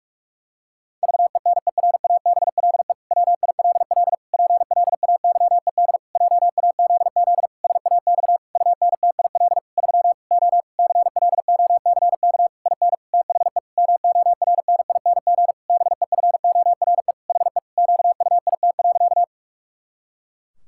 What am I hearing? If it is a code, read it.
Morse code, heard as a keyed tone, about 46 wpm, 700 Hz. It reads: VENERABLE WILL PLAYED JAZZ SAX UNTIL 3 O CLOCK IN THE MORNING BEFORE HE QUIT.
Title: Pangram Challenge v2 - 45wpm